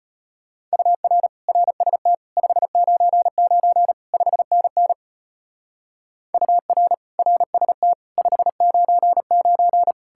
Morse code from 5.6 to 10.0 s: UR RST 599

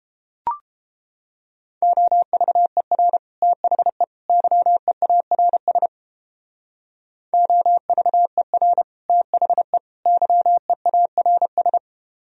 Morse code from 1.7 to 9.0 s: OVER THE YEARS OVER

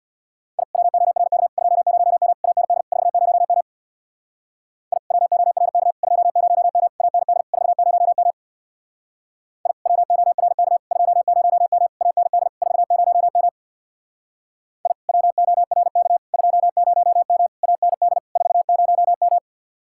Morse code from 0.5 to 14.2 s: I WORK 20M AND 40M I WORK 20M AND 40M I WORK 20M AND 40M

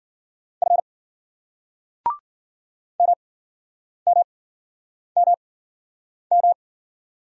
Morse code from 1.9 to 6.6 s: MMMM

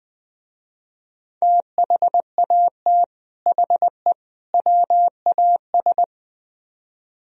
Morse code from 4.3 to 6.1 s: WAS

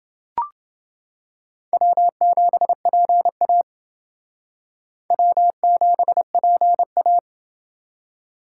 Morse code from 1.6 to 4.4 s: W7PA

